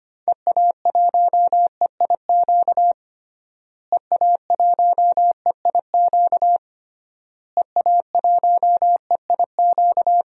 Morse code, heard as a keyed tone, about 25 wpm, 700 Hz. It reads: EA1EIQ EA1EIQ EA1EIQ
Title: DXCC-100 Call Signs - 6 Characters - Repeated 3x - 25wpm